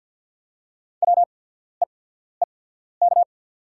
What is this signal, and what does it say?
Morse code, heard as a keyed tone, about 50 wpm, spaced as for 14 wpm, 690 Hz. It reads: WEEK
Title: 8x Character Spacing - Top 500 Words - 50wpm